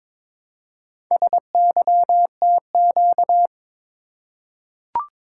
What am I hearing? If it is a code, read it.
Morse code, heard as a keyed tone, about 22 wpm, 700 Hz. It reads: SYTQ
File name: practice / 22wpm / Four Letter-Number Combinations - 22wpm